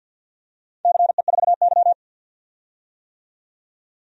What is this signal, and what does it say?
Morse code, heard as a keyed tone, about 49 wpm, 700 Hz. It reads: CE4Y
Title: DXCC-100 Call Signs - 4 Characters - 50wpm